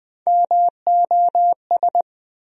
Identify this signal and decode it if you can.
Morse code, heard as a keyed tone, about 20 wpm, 700 Hz. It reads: MOS
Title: USA Call Sign Postfixes - 3 Letters 20wpm